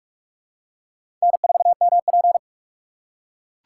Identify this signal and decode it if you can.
Morse code, heard as a keyed tone, about 44 wpm, 700 Hz. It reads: N4MP
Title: USA Call Signs - 4x Character Spacing - 45wpm